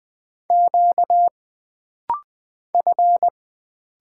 Morse code, heard as a keyed tone, about 20 wpm, 700 Hz. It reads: Q F
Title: Single Letter-Number Alt - Rapid-Fire 20wpm